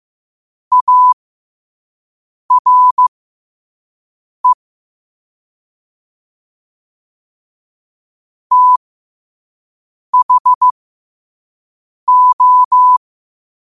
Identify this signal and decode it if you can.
Morse code, heard as a keyed tone, about 15 wpm, spaced as for 5 wpm, 1000 Hz. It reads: ARE THO
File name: qotdfiles / 5wpm